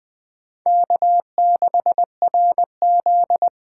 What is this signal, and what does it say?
Morse code, heard as a keyed tone, about 21 wpm, 700 Hz. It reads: K6RZ
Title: Speed Racing - USA 1x2 Call Signs 17wpm